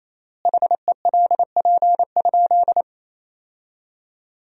Morse code, heard as a keyed tone, about 28 wpm, 700 Hz. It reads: HELP?
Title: 3x Word-Spacing - Sentences from Top 200 Words - 28wpm